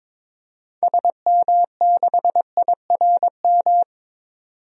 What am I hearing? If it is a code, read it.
Morse code, heard as a keyed tone, about 22 wpm, 700 Hz. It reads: SM6IRM
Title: DXCC-100 Call Signs - 6 Characters - 22wpm